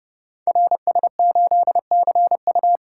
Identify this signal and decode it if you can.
Morse code, heard as a keyed tone, about 30 wpm, 700 Hz. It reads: RS8CU
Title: Code Groups - 5 Letter:Numbers - 30wpm